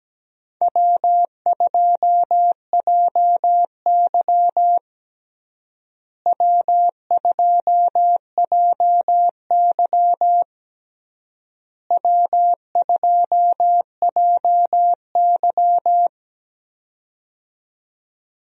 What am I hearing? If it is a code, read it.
Morse code, heard as a keyed tone, about 17 wpm, 700 Hz. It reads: W2JY W2JY W2JY
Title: USA Call Signs - 4 Characters - Repeated 3x - 17wpm